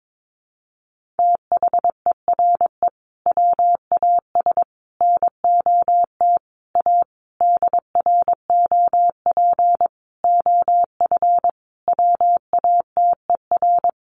Morse code, heard as a keyed tone, about 22 wpm, 700 Hz. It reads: THERE WAS NOT A DROP OF WATER
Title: Sentences from Top 1400 Words 22wpm